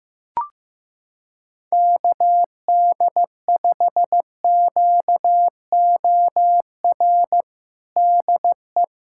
Morse code, heard as a keyed tone, about 15 wpm, 700 Hz. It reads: KD5QOR DE
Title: USA Callsign DE Callsign - 6 Characters - 15wpm